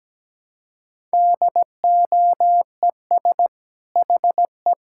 Morse code, heard as a keyed tone, about 17 wpm, 700 Hz. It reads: DOES HE
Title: Sentences from Top 1000 Words - Review Part 5 - 17wpm